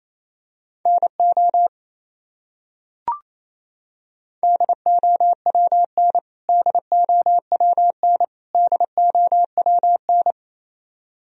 Morse code, heard as a keyed tone, about 28 wpm, 700 Hz. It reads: NO DOWN DOWN DOWN
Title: Top 100 Words Repeated 28wpm